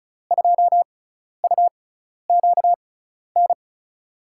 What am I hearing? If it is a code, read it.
Morse code, heard as a keyed tone, about 36 wpm, spaced as for 12 wpm, 700 Hz. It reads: 2UQN
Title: USA Call Signs - 6 Characters - 6x Character Spacing - 35wpm